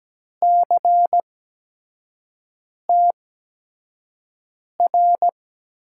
Morse code, heard as a keyed tone, about 17 wpm, spaced as for 5 wpm, 700 Hz. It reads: CTR
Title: USA Call Signs - 6 Characters - 8x Character Spacing - 17wpm